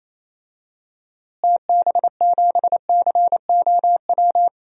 Morse code, heard as a keyed tone, about 27 wpm, 700 Hz. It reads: TB7COW
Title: DXCC-100 Call Signs - 6 Characters - Repeated 2x - 28wpm